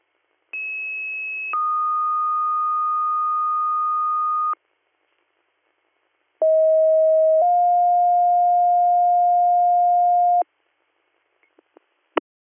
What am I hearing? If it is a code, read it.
Two-tone page, heard as a one-second tone, then a three-second tone, about 2600 then 1230 Hz, about 640 then 710 Hz. It is heard from 0.5 to 10.4 s.